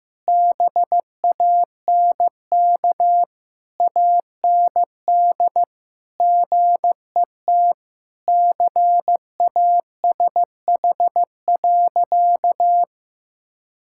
Morse code, heard as a keyed tone, about 15 wpm, 700 Hz. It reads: BANK AND GET CASH.
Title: Sentences from Top 1000 Words - Review Part 5 - 15wpm